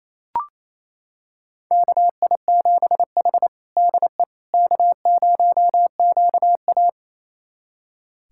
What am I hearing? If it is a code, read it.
Morse code, heard as a keyed tone, about 28 wpm, 700 Hz. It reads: KI7H DE K0QA
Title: USA Callsign DE Callsign - 4 Characters - 28wpm